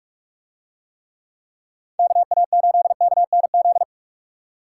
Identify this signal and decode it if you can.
Morse code, heard as a keyed tone, about 44 wpm, 700 Hz. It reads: KA8KNZ